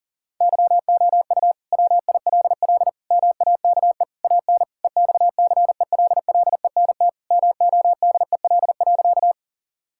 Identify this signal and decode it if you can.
Morse code, heard as a keyed tone, about 37 wpm, 700 Hz. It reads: YOU WILL MAKE AN EXCELLENT MODEL.
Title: Sentences from Top 1000 Words - Review Part 5 - 40wpm